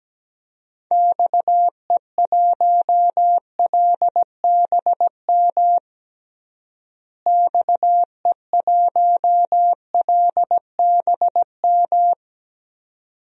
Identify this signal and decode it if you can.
Morse code, heard as a keyed tone, about 17 wpm, 700 Hz. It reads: XE1LBM XE1LBM
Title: DXCC-100 Call Signs - 6 Characters - Repeated 2x - 17wpm